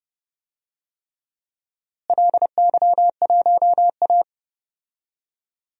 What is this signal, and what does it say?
Morse code, heard as a keyed tone, about 30 wpm, 700 Hz. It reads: LY1A
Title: DXCC-100 Call Signs - 4 Characters - Repeated 3x - 30wpm